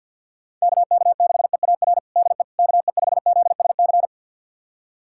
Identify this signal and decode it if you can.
Morse code, heard as a keyed tone, about 51 wpm, 700 Hz. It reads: KK6EAR DE KE5ZSC
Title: USA Callsign DE Callsign 50wpm